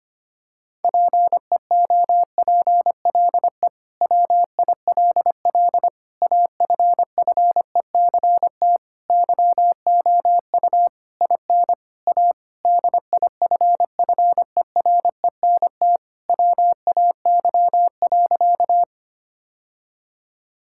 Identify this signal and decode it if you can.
Morse code, heard as a keyed tone, about 25 wpm, 700 Hz. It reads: PEOPLE WILL AFFECT YOU IN A DIFFERENT WAY.
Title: Sentences from Top 1000 Words - Review Part 6 - 25wpm